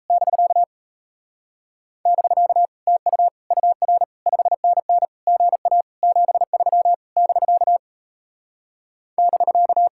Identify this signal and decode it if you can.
Morse code, heard as a keyed tone, about 38 wpm, 700 Hz. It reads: <BK> <BK> TU UR 5NN GA 73 <BK> <BK>